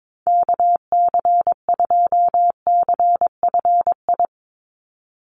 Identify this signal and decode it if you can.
Morse code, heard as a keyed tone, about 22 wpm, 700 Hz. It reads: KC2CFI
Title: USA Call Signs 22wpm